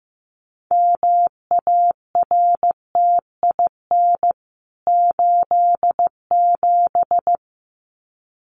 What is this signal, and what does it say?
Morse code, heard as a keyed tone, about 15 wpm, 700 Hz. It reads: MARTIN 87
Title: CWT Exchange 15wpm